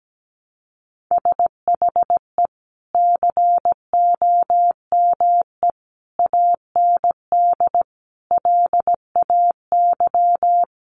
Morse code, heard as a keyed tone, about 17 wpm, 700 Hz. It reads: SHE COME AND LAY
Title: Sentences from Top 1100 Words 17wpm